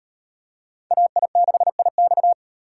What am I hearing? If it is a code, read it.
Morse code, heard as a keyed tone, about 36 wpm, 700 Hz. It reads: AI6IX